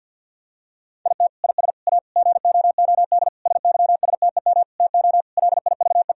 Morse code, heard as a keyed tone, about 50 wpm, 700 Hz. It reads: IT IS A GOOD SYSTEM TO LIVE
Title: Sentences from Top 1000 Words - Review Part 3 - 50wpm